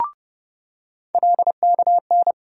Morse code, heard as a keyed tone, about 30 wpm, 700 Hz. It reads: LKN
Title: USA Call Sign Postfixes - 3 Letters 30wpm